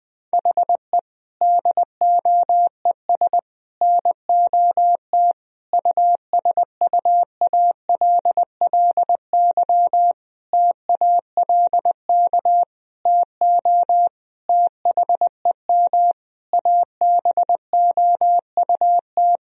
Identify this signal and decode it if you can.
Morse code, heard as a keyed tone, about 20 wpm, 700 Hz. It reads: HE DOES NOT USUALLY TALK TO THEM ABOUT
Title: Sentences from Top 1000 Words - Review Part 3 - 20wpm